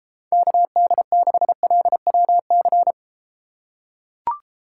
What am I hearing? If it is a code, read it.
Morse code, heard as a keyed tone, about 30 wpm, 700 Hz. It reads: KD6LWC